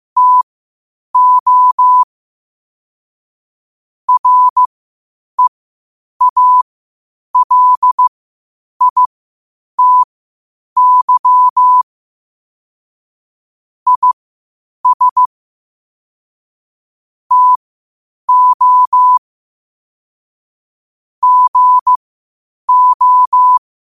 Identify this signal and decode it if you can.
Morse code, heard as a keyed tone, about 15 wpm, spaced as for 8 wpm, 1000 Hz. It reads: TO REALITY IS TO GO
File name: qotdfiles / 8wpm